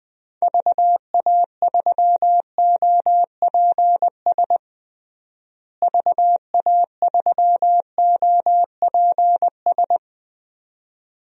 Morse code, heard as a keyed tone, about 20 wpm, 700 Hz. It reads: VA3OPS VA3OPS
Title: DXCC-100 Call Signs - 6 Characters - Repeated 2x - 20wpm